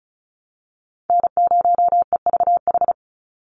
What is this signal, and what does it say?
Morse code, heard as a keyed tone, about 34 wpm, 700 Hz. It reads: N0EVH
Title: NA SOTA Chaser Callsigns 35wpm